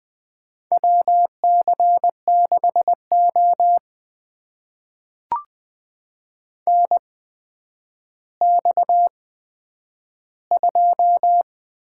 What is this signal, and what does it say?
Morse code, heard as a keyed tone, about 20 wpm, spaced as for 10 wpm, 700 Hz. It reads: WC6O NX2
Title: USA Call Signs - 4 Characters - 8x Character Spacing - 20wpm